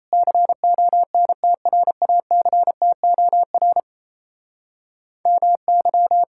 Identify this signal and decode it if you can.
Morse code, heard as a keyed tone, about 38 wpm, 700 Hz. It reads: CONTRACTOR MY